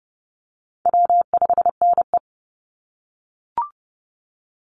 Morse code, heard as a keyed tone, about 30 wpm, 700 Hz. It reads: W5NE
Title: Call Signs - 30wpm